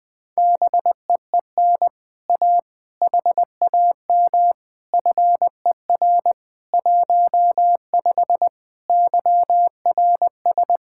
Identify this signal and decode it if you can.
Morse code, heard as a keyed tone, about 21 wpm, 700 Hz. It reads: BEEN A HAM FER 15 YRS
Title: Speed-Racing Common QSO Phrases - 20wpm